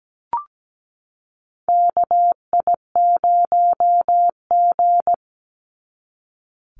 Morse code, heard as a keyed tone, about 17 wpm, 700 Hz. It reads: KI0G
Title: NA SOTA Chaser Callsigns 17wpm